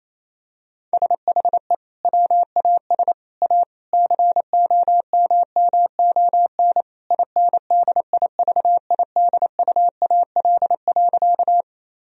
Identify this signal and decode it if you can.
Morse code, heard as a keyed tone, about 28 wpm, 700 Hz. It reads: SHE WAS A COMMON INDIVIDUAL.
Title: Sentences from Top 1000 Words - Review Part 4 - 28wpm